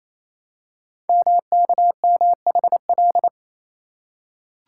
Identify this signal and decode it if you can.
Morse code, heard as a keyed tone, about 27 wpm, 700 Hz. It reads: MKMHL
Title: Code Groups - 5 Letters - 28wpm